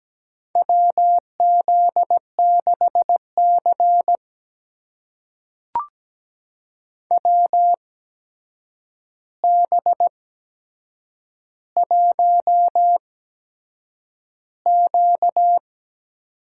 Morse code, heard as a keyed tone, about 17 wpm, spaced as for 7 wpm, 700 Hz. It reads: WZ6C WB1Q
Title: USA Call Signs - 4 Characters - 8x Character Spacing - 17wpm